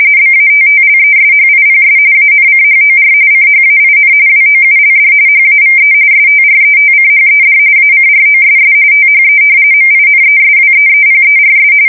RTTY, about 45.45 baud. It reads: BATTLESHIPS ANCHOR CHAIN WOULD HAVE STRETCHED ACROSS THE MINESWEEPERS
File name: fsk170 / battleshipnj